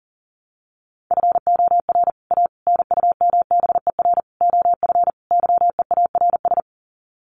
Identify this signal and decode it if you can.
Morse code, heard as a keyed tone, about 40 wpm, 700 Hz. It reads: FOR A NUMBER OF YEARS
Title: Sets of 5 Words Encore 40wpm